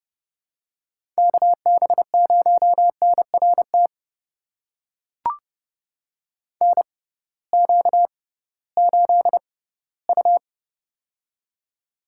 Morse code, heard as a keyed tone, about 30 wpm, spaced as for 18 wpm, 700 Hz. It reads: KB0NRT NQ8U
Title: USA Call Signs - 6x Character Spacing - 30wpm